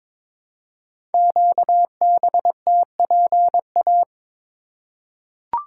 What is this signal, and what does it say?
Morse code, heard as a keyed tone, about 22 wpm, 700 Hz. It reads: QBTPA
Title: Code Groups - 5 Letters - 22wpm